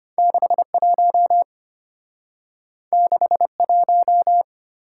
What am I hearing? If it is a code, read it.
Morse code, heard as a keyed tone, about 34 wpm, 700 Hz. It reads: 61 61